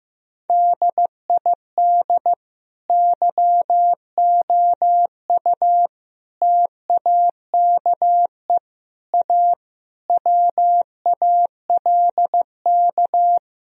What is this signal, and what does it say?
Morse code, heard as a keyed tone, about 15 wpm, 700 Hz. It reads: DID YOU TAKE A WALK
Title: Sentences from Top 1000 Words - Review Part 4 - 15wpm